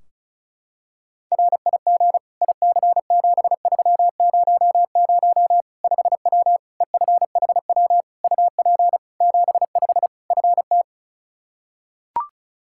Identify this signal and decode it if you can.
Morse code, heard as a keyed tone, about 34 wpm, 700 Hz. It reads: RIG IC7300 5W EFHW UP 75 FT
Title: Practice QSO - Exchange 2 - Info - 35wpm